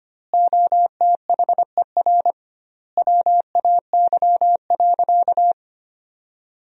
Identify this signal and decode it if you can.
Morse code, heard as a keyed tone, about 25 wpm, 700 Hz. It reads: OTHER WAY.
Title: 2x Word-Spacing - Sentences from Top 100 Words - 25wpm